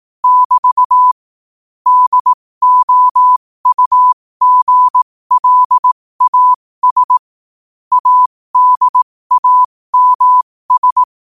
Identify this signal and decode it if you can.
Morse code, heard as a keyed tone, about 18 wpm, spaced as for 15 wpm, 1000 Hz. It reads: <BT> DOUGLAS ADAMS
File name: qotdfiles / 15wpm